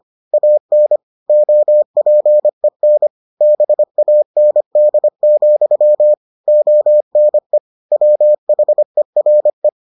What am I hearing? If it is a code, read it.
Morse code, heard as a keyed tone, about 25 wpm, 600 Hz. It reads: AN OPEN BAND, ONE WHERE